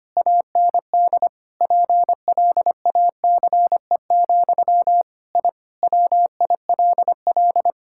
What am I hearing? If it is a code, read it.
Morse code, heard as a keyed tone, about 25 wpm, 700 Hz. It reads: AND PLACE, I WILL
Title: Sentences from Top 1000 Words - Review Part 5 - 25wpm